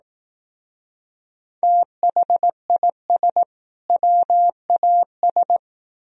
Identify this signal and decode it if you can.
Morse code, heard as a keyed tone, about 18 wpm, 700 Hz. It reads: THIS WAS
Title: Speed-Racing - Sets of 2 Words 15wpm